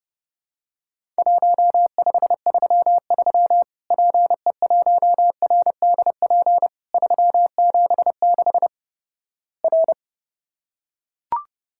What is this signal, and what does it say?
Morse code, heard as a keyed tone, about 30 wpm, 700 Hz. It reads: 1533 PE1RDP 376 R
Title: Worked All Europe DX Contest - QTC Exchange - 30wpm